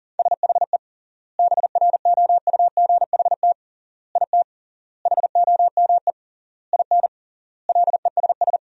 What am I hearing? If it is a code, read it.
Morse code, heard as a keyed tone, about 40 wpm, 700 Hz. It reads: SHE BROUGHT IT HOME IN LESS
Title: 3x Word-Spacing - Sentences from Top 500 Words - 40wpm